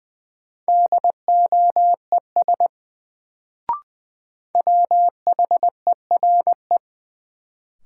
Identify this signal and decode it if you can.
Morse code, heard as a keyed tone, about 20 wpm, 700 Hz. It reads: DOES WHERE
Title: Sets of 2 Words - New Word Drill - 20wpm